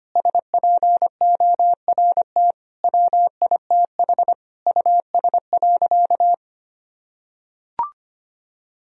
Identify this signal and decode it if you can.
Morse code, heard as a keyed tone, about 25 wpm, 700 Hz. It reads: SPORT WITH US.
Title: Sentences from Top 1000 Words - Review Part 4 - 25wpm